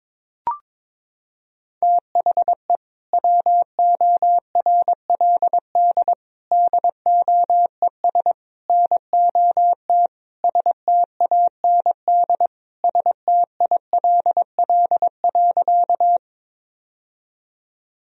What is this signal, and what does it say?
Morse code, heard as a keyed tone, about 22 wpm, 700 Hz. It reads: THE WORLD DOES NOT STAND STILL.
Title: Sentences from Top 1000 Words - Review Part 1 - 22wpm